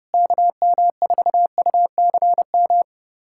Morse code, heard as a keyed tone, about 30 wpm, 690 Hz. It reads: KM4UCM
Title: USA Callsign DE Callsign - 6 Characters - 30wpm